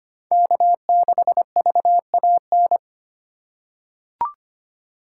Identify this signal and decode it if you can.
Morse code, heard as a keyed tone, about 25 wpm, 700 Hz. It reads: K6VAN
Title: USA Call Signs - 5 Characters - 2x Character Spacing - 25wpm